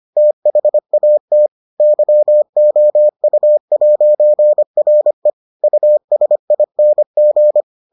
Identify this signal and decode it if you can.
Morse code, heard as a keyed tone, about 25 wpm, 600 Hz. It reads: THAT YOU'RE USING